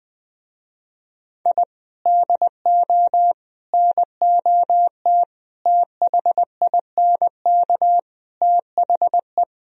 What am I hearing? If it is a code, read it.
Morse code, heard as a keyed tone, about 20 wpm, 700 Hz. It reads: I DO NOT THINK THE
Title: Speed-Racing - Sets of 5 Words 20wpm